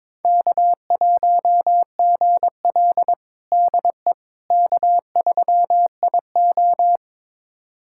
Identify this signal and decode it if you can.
Morse code, heard as a keyed tone, about 22 wpm, 700 Hz. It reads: K1GL DE K3IO
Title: USA Callsign DE Callsign - 4 Characters - 22wpm